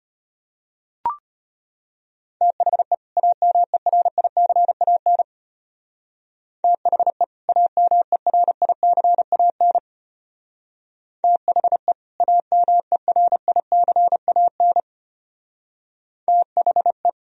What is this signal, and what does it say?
Morse code, heard as a keyed tone, about 32 wpm, 700 Hz. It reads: THE AMERICAN THE AMERICAN THE AMERICAN THE